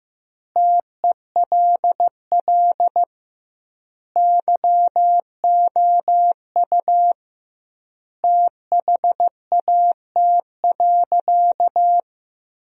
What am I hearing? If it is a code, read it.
Morse code, heard as a keyed tone, about 15 wpm, 700 Hz. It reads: TELL YOU THAT.
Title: 2x Word-Spacing - Sentences from Top 300 Words - 15wpm